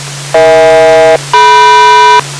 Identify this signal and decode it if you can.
Aircraft SELCAL, heard as two pairs of tones. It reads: EH-DM